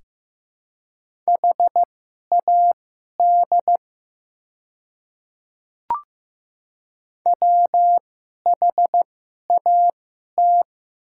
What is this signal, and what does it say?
Morse code, heard as a keyed tone, about 15 wpm, spaced as for 11 wpm, 700 Hz. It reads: HAD WHAT
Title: 2x Character Spacing - Top 100 Words - 15wpm